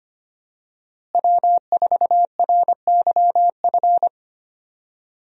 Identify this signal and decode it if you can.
Morse code, heard as a keyed tone, about 25 wpm, 700 Hz. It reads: W4RYF
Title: USA Call Signs - 5 Characters - Repeated 2x - 25wpm